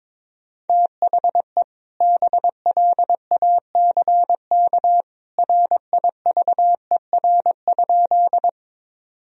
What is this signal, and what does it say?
Morse code, heard as a keyed tone, about 22 wpm, 700 Hz. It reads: THE BLACK RIVER?
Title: Sentences from Top 1000 Words - Review Part 2 - 22wpm